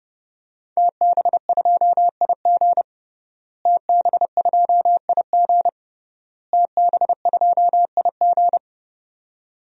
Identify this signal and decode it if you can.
Morse code, heard as a keyed tone, about 32 wpm, 700 Hz. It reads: TB2IG TB2IG TB2IG
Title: DXCC-100 Call Signs - 5 Characters - Repeated 3x - 30wpm